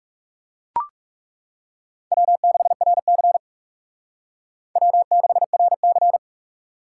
Morse code, heard as a keyed tone, about 38 wpm, 700 Hz. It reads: W6RC W6RC